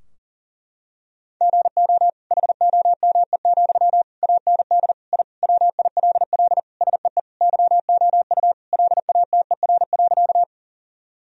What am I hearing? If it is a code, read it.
Morse code, heard as a keyed tone, about 37 wpm, 700 Hz. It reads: GO HOME, AND I WILL SEE YOU LATER.
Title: Sentences from Top 1000 Words - Review Part 4 - 40wpm